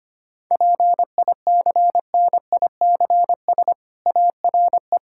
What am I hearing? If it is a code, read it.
Morse code, heard as a keyed tone, about 25 wpm, 700 Hz. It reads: PICNICS ARE
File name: practice / 25wpm / Difficult Sentences with Hints - 25wpm